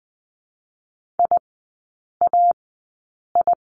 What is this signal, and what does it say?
Morse code, heard as a keyed tone, about 20 wpm, spaced as for 8 wpm, 690 Hz. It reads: IAI
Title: 087 Lesson 15e - Number 5 - Three Characters 20wpm